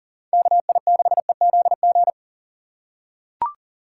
Morse code, heard as a keyed tone, about 40 wpm, 700 Hz. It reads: KI6EZG
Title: USA Call Signs - 6 Characters - 6x Character Spacing - 40wpm